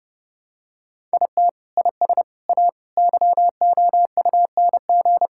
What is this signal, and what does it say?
Morse code, heard as a keyed tone, about 30 wpm, 700 Hz. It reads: IT IS A YOUNG
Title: Sentences from Top 1000 Words - Review Part 6 - 30wpm